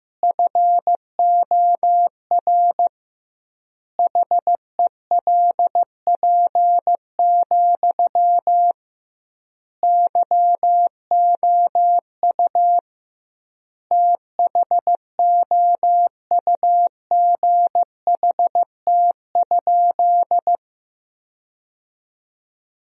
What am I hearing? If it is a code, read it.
Morse code, heard as a keyed tone, about 16 wpm, 700 Hz. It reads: FOR HELP, YOU THOUGHT?
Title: 2x Word-Spacing - Sentences from Top 300 Words - 15wpm